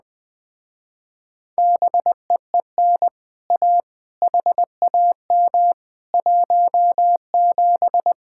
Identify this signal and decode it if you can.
Morse code, heard as a keyed tone, about 20 wpm, 700 Hz. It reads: BEEN A HAM 17
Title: Speed-Racing Common QSO Phrases - 17wpm